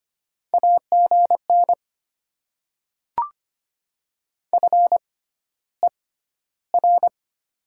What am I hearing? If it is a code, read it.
Morse code, heard as a keyed tone, about 25 wpm, spaced as for 13 wpm, 690 Hz. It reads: AGN FER
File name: Top 20 QSO Elements - 6x Character Spacing - 25wpm